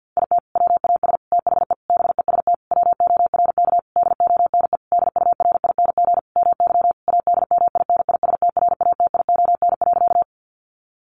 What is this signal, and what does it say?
Morse code, heard as a keyed tone, about 50 wpm, 700 Hz. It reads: IT WAS THE BEST WORK DONE DURING MY ADMINISTRATION.
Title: Sentences from Top 1300 Words 50wpm